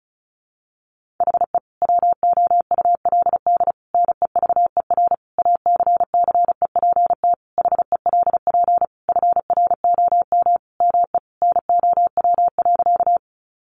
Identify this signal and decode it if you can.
Morse code, heard as a keyed tone, about 34 wpm, 700 Hz. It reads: HE WOULD NEVER ACCEPT HELP FROM ME NOW.
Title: Sentences from Top 1300 Words 35wpm